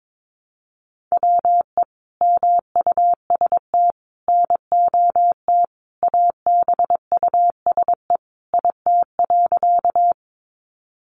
Morse code, heard as a keyed tone, about 22 wpm, 700 Hz. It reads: WE MUST NOT ABUSE IT.
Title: Sentences from Top 1300 Words 22wpm